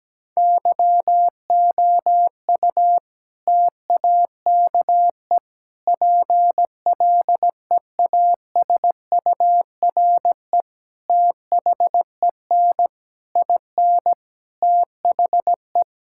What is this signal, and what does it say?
Morse code, heard as a keyed tone, about 17 wpm, 700 Hz. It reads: YOU TAKE PLEASURE THEN IN THE
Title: Sentences from Top 1600 Words 17wpm